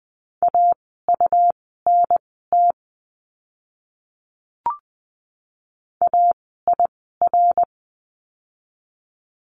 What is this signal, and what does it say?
Morse code, heard as a keyed tone, about 20 wpm, spaced as for 14 wpm, 700 Hz. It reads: AUNT AIR
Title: 101 Lesson 17f - Letter u - Words 20wpm